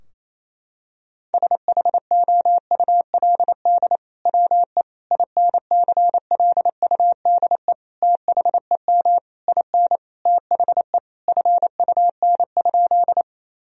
Morse code, heard as a keyed tone, about 28 wpm, 700 Hz. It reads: SHOULD WE INCLUDE THEM IN THE FUN?
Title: Sentences from Top 1000 Words - Review Part 3 - 28wpm